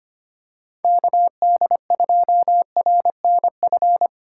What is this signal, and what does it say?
Morse code, heard as a keyed tone, about 25 wpm, 700 Hz. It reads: KD2RNF
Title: USA Call Signs - 6 Characters - 2x Character Spacing - 25wpm